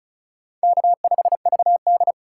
Morse code, heard as a keyed tone, about 36 wpm, 700 Hz. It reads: K5VD
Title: USA Call Signs - 2x Character Spacing - 35wpm